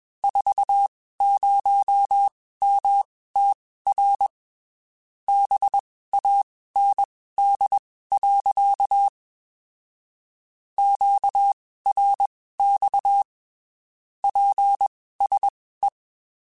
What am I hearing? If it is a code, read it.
Morse code, heard as a keyed tone, about 21 wpm, spaced as for 15 wpm, 800 Hz. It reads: 40MTR BAND. QRX PSE
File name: (spaced) / QSOs Spaced-mp3s / WA7QOX 2 @20 sp